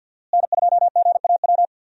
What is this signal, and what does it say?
Morse code, heard as a keyed tone, about 52 wpm, 700 Hz. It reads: N2GAW
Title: USA Call Signs - 5 Characters - Repeated 3x - 50wpm